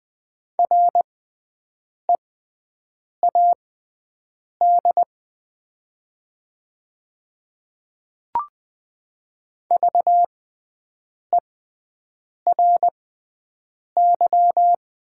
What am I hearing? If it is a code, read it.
Morse code, heard as a keyed tone, about 20 wpm, spaced as for 7 wpm, 700 Hz. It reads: READ VERY
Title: 6x Character Spacing - Top 200 Words - 20wpm